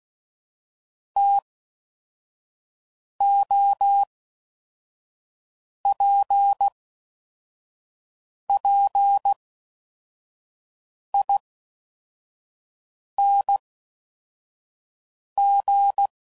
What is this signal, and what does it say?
Morse code, heard as a keyed tone, about 16 wpm, spaced as for 4 wpm, 800 Hz. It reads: TOPPING
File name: Mixed / Variable Sp / TOs